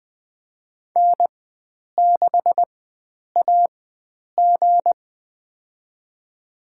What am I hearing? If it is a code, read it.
Morse code, heard as a keyed tone, about 21 wpm, spaced as for 9 wpm, 700 Hz. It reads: N6AG
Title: USA Call Signs - 4x Character Spacing - 20wpm